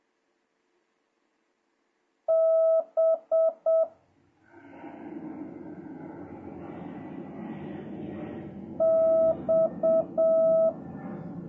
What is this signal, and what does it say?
Morse code, heard as a keyed tone, about 7 wpm, 1300 Hz, with noise. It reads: B X
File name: jh8cbh / 2026cw